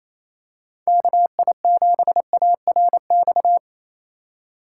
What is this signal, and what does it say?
Morse code, heard as a keyed tone, about 28 wpm, 700 Hz. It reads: KI7ARX
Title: USA Call Signs - 6 Characters - 3x Character Spacing - 28wpm